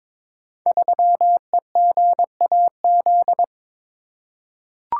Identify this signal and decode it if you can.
Morse code, heard as a keyed tone, about 22 wpm, 700 Hz. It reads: 3EGAZ
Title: Code Groups - 5 Letter:Numbers - 22wpm